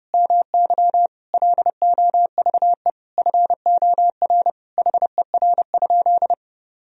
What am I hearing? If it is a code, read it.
Morse code, heard as a keyed tone, about 30 wpm, 700 Hz. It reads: MY LOVE FOR HER?
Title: Sentences from Top 1000 Words - Review Part 1 - 30wpm